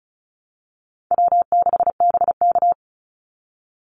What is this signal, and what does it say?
Morse code, heard as a keyed tone, about 34 wpm, 700 Hz. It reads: W6BK
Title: Easy Call Signs Repeated 3x - 35wpm